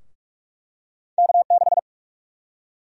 Morse code, heard as a keyed tone, about 44 wpm, 700 Hz. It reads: K6
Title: USA Call Sign Prefixes - 45wpm